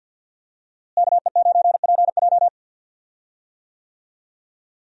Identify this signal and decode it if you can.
Morse code, heard as a keyed tone, about 47 wpm, 700 Hz. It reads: KE9PJ
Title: USA Call Signs - 5 Characters - Repeated 2x - 50wpm